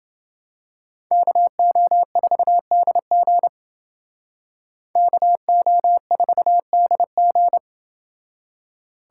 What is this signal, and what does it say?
Morse code, heard as a keyed tone, about 28 wpm, 700 Hz. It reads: KO4DG KO4DG